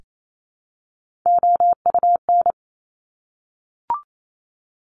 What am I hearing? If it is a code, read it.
Morse code, heard as a keyed tone, about 28 wpm, 700 Hz. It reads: OUN
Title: Top Three Letter Combinations in Words 28wpm